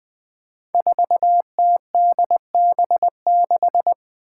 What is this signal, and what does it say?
Morse code, heard as a keyed tone, about 20 wpm, 700 Hz. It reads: 4TDB6
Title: Five Challenging Letter-Number Combinations 20wpm